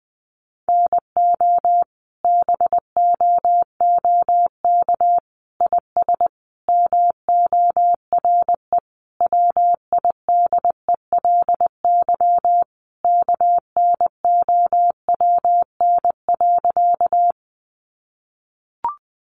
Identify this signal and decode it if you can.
Morse code, heard as a keyed tone, about 20 wpm, 700 Hz. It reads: NO BOOK IS MORE WIDELY KNOWN.
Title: Sentences from Top 1200 Words 20wpm